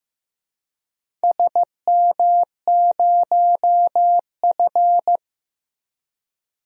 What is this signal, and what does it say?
Morse code, heard as a keyed tone, about 15 wpm, 700 Hz. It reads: SM0F
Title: DXCC-100 Call Signs - 4 Characters - Repeated 2x - 15wpm